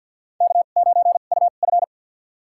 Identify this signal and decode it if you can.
Morse code, heard as a keyed tone, about 46 wpm, spaced as for 34 wpm, 700 Hz. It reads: K9UF
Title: USA Call Signs - 4 Characters - 2x Character Spacing - 50wpm